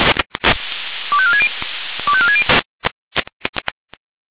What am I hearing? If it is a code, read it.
Five-tone selective call (ZVEI), heard as quick runs of five tones, tones of about 70 ms. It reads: 25460 25460